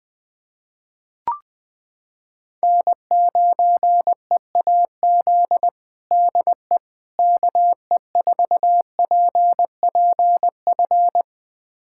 Morse code, heard as a keyed tone, about 20 wpm, 700 Hz. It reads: N9EAZ DE KE4PPF
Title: USA Callsign DE Callsign 20wpm